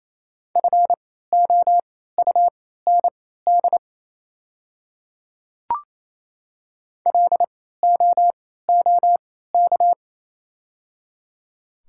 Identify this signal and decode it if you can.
Morse code, heard as a keyed tone, about 28 wpm, spaced as for 16 wpm, 700 Hz. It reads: FOUND LOOK
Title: 3x Character Spacing - Top 300 Words - 28wpm